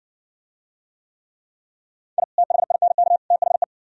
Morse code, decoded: I THINK THE

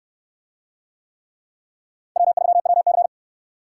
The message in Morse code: W3WC

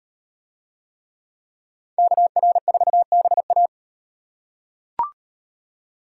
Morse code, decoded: KR4BA

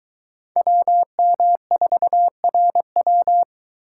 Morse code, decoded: WM4RW

